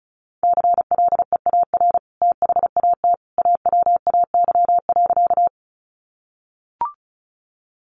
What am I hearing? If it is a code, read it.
Morse code, heard as a keyed tone, about 34 wpm, 700 Hz. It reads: CLEAR THAT AWAY.
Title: Sentences from Top 600 Words 35wpm